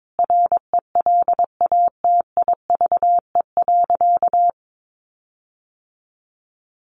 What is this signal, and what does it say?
Morse code, heard as a keyed tone, about 22 wpm, 700 Hz. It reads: RELATIVE.
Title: Sentences from Top 1400 Words 22wpm